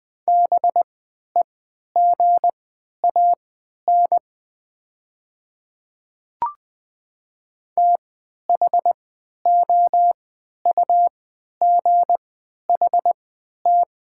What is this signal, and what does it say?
Morse code, heard as a keyed tone, about 20 wpm, spaced as for 11 wpm, 700 Hz. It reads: BEGAN THOUGHT
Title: 3x Character Spacing - Top 300 Words - 20wpm